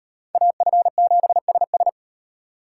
Morse code, heard as a keyed tone, about 38 wpm, 700 Hz. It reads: AF7SS